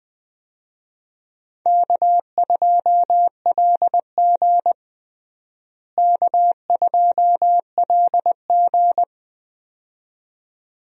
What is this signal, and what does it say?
Morse code, heard as a keyed tone, about 20 wpm, 700 Hz. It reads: K2LG K2LG
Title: USA Call Signs - 4 Characters - Repeated 2x - 20wpm